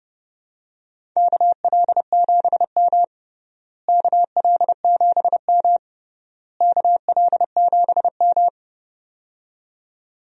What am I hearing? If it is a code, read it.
Morse code, heard as a keyed tone, about 30 wpm, 700 Hz. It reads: KL7M KL7M KL7M
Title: DXCC-100 Call Signs - 4 Characters - Repeated 3x - 30wpm